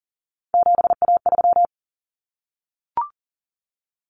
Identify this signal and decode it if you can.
Morse code, heard as a keyed tone, about 40 wpm, 700 Hz. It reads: 7A3
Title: Common Cut Numbers - 3-Digit 40wpm